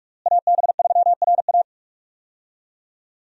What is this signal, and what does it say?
Morse code, heard as a keyed tone, about 47 wpm, 700 Hz. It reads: AB3RA
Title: USA Call Signs - 5 Characters - Repeated 3x - 45wpm